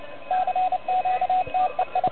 Morse code, heard as a keyed tone, about 27 wpm, 740 Hz, with radio noise. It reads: CQ TEI
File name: truSDX / MWC29.8